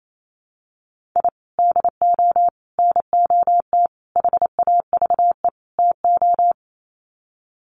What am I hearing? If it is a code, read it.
Morse code, heard as a keyed tone, about 28 wpm, 700 Hz. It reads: I DO NOT HAVE TO
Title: Sets of 5 Words 28wpm